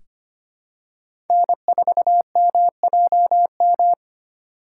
Morse code, heard as a keyed tone, about 25 wpm, 700 Hz. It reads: N4MJM
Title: USA Call Signs - 5 Characters - 4x Character Spacing - 25wpm